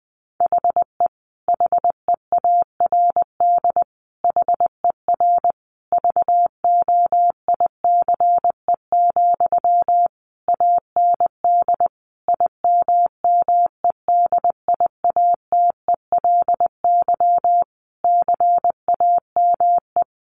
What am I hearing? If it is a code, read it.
Morse code, heard as a keyed tone, about 19 wpm, 700 Hz. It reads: HE HEARD HER VOICE, AND IMMEDIATELY CAME
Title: Sentences from Top 1100 Words 20wpm